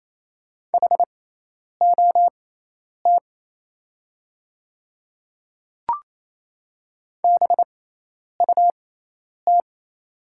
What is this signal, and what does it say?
Morse code, heard as a keyed tone, about 27 wpm, spaced as for 10 wpm, 700 Hz. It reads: HOT BUT
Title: 6x Character Spacing - Top 100 Words - 28wpm